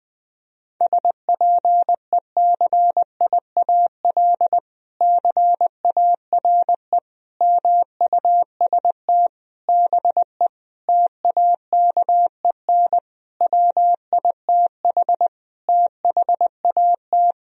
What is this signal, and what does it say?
Morse code, heard as a keyed tone, about 20 wpm, 700 Hz. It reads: SPECIAL CARE MUST BE TAKEN WITH THAT
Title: Sentences from Top 1000 Words - Review Part 4 - 20wpm